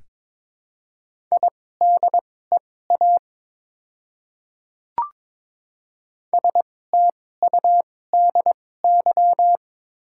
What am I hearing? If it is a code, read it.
Morse code, heard as a keyed tone, about 22 wpm, spaced as for 16 wpm, 690 Hz. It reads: IDEA STUDY
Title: 2x Character Spacing - Top 500 Words - 22wpm